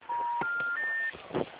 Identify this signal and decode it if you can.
Special information tone, with noise.
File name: out- -2007- - - .0